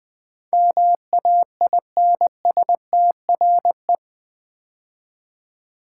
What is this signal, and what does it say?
Morse code, heard as a keyed tone, about 20 wpm, 700 Hz. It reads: MAINSTRE
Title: Build A Word - 4 Letters At A Time - 10-Letter Words - 20wpm